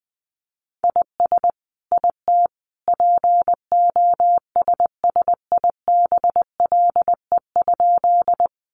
Morse code, heard as a keyed tone, about 20 wpm, 700 Hz. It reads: IS IT POSSIBLE?
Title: Sentences from Top 500 Words 20wpm